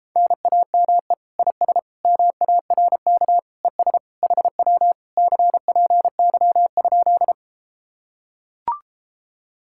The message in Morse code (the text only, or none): NAME IS MARK ES HW CPY?